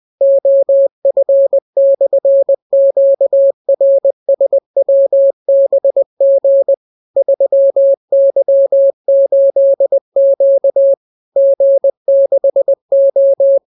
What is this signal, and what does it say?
Morse code, heard as a keyed tone, about 20 wpm, 550 Hz. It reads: OF/QRSWBG 3Y8Q G6O